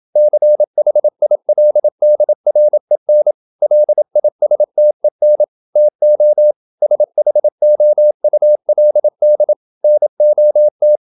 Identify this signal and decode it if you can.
Morse code, heard as a keyed tone, about 27 wpm, 600 Hz. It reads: CHILDREN LISTEN TO SHOULD NOT
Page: Never 26wpm | CW med Gnister